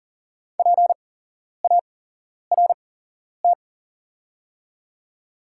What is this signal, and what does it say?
Morse code, heard as a keyed tone, about 41 wpm, spaced as for 11 wpm, 700 Hz. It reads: PART
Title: 8x Character Spacing - Top 500 Words - 40wpm